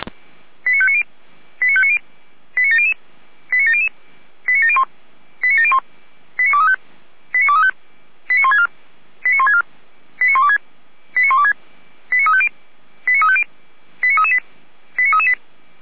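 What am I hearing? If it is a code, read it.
Five-tone selective call (ZVEI), heard as quick runs of five tones, tones of about 70 ms. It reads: 89590 89590 89700 89700 89701 89701 89235 89235 89164 89164 89126 89126 89350 89350 89338 89338
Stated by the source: ZVEI five-tone call 89094 received on SDR.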